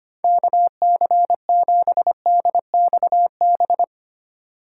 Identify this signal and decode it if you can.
Morse code, heard as a keyed tone, about 25 wpm, 700 Hz. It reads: KC7DXB
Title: USA Call Signs - 6 Characters - 8x Character Spacing - 25wpm